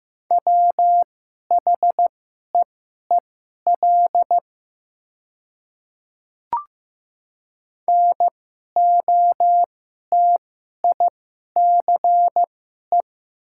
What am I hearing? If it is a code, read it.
Morse code, heard as a keyed tone, about 15 wpm, spaced as for 11 wpm, 700 Hz. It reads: WHEEL NOTICE
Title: 2x Character Spacing - Top 500 Words - 15wpm